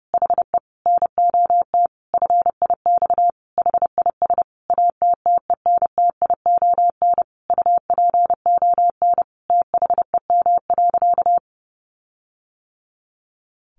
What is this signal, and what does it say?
Morse code, heard as a keyed tone, about 30 wpm, 700 Hz. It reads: HE NOT FIX HIS ATTENTION UPON THEM.
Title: Sentences from Top 1400 Words 30wpm